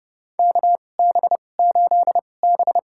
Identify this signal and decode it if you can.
Morse code, heard as a keyed tone, about 30 wpm, spaced as for 22 wpm, 700 Hz. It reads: KB8B
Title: USA Call Signs - 4 Characters - 2x Character Spacing - 30wpm